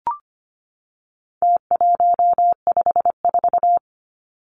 Morse code, heard as a keyed tone, about 25 wpm, 700 Hz. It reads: T154
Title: Numbers - Contest Style - 4-Digit 25wpm